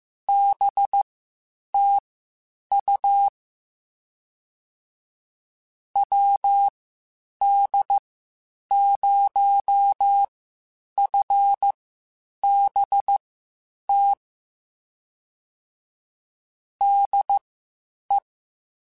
Morse code, heard as a keyed tone, about 15 wpm, spaced as for 8 wpm, 800 Hz. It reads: BTU WD0FBT DE
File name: QSOs / QSOs-mp3s / WD0FBT 1@13